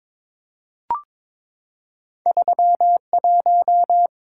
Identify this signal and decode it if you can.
Morse code, heard as a keyed tone, about 22 wpm, 700 Hz. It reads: 31
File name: practice / 22wpm / Two Letter-Number Combinations - 22wpm